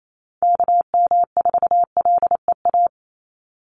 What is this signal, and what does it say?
Morse code, heard as a keyed tone, about 28 wpm, 700 Hz. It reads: KM4LEA
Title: USA Call Signs 28wpm